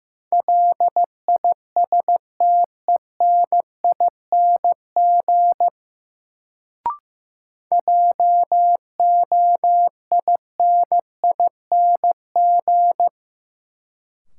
Morse code, heard as a keyed tone, about 16 wpm, 700 Hz. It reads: LISTENING JOINING
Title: Sets of 5 Words - New Word Drill - 15wpm